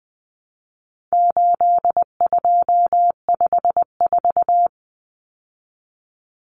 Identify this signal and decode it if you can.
Morse code, heard as a keyed tone, about 20 wpm, 700 Hz. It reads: 8254
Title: Common Cut Numbers - 4-Digit 20wpm